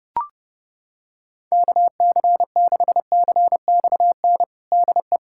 Morse code, heard as a keyed tone, about 30 wpm, 700 Hz. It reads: KC6CXN DE
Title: USA Callsign DE Callsign - 6 Characters - 30wpm